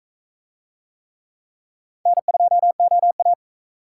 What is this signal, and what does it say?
Morse code, heard as a keyed tone, about 38 wpm, 700 Hz. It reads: N2OA